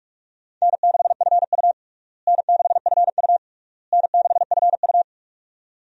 Morse code, heard as a keyed tone, about 42 wpm, 700 Hz. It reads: N6FU N6FU N6FU
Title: USA Call Signs - 4 Characters - Repeated 3x - 45wpm